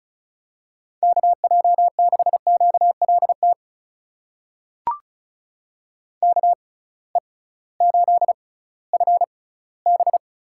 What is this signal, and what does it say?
Morse code, heard as a keyed tone, about 35 wpm, spaced as for 19 wpm, 700 Hz. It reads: KJ6QLT KE8FB
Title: USA Call Signs - 6x Character Spacing - 35wpm